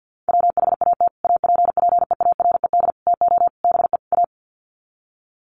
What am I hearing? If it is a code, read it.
Morse code, heard as a keyed tone, about 49 wpm, 700 Hz. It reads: WHAT APPEARED TO BE A
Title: Sets of 5 Words Encore2 50wpm